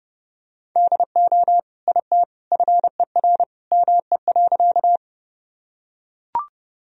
Morse code, heard as a keyed tone, about 30 wpm, 700 Hz. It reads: DO IT FER ME.
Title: Sentences from Top 200 Words and Abbreviations 30wpm